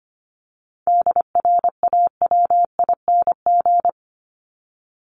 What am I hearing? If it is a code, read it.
Morse code, heard as a keyed tone, about 25 wpm, 700 Hz. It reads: DRAWING
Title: ING Suffix - Top 100 Words with Base Words 25wpm